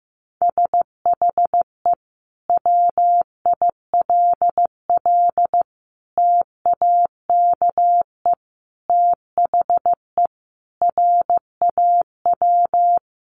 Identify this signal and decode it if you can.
Morse code, heard as a keyed tone, about 15 wpm, 700 Hz. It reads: SHE WILL TAKE THE RAW
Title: Sentences from Top 1500 Words 15wpm